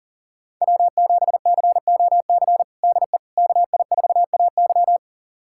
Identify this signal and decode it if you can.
Morse code, heard as a keyed tone, about 40 wpm, 700 Hz. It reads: W7COC DE KI4AY
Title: USA Callsign DE Callsign - 5 Characters - 40wpm